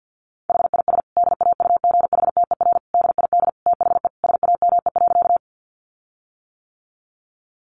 Morse code, heard as a keyed tone, about 50 wpm, 700 Hz. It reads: HIS DAUGHTER DID THE SAME.
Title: Sentences from Top 1500 Words 50wpm